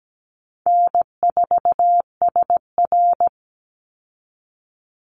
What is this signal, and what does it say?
Morse code, heard as a keyed tone, about 17 wpm, 700 Hz. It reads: N4SR
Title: Easy Call Signs Repeated 3x - 17wpm